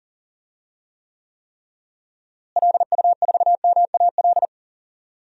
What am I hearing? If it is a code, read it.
Morse code, heard as a keyed tone, about 40 wpm, 700 Hz. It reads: LU4MAL
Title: DXCC-100 Call Signs - 6 Characters - Repeated 3x - 40wpm